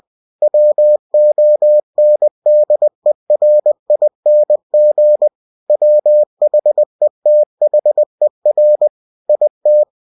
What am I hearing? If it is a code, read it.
Morse code, heard as a keyed tone, about 20 wpm, 600 Hz. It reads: WONDERING WHETHER IT
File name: cw-books / winnie-the-pooh / winnie-20wpm-page-0078